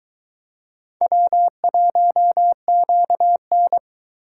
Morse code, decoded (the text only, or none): W1QN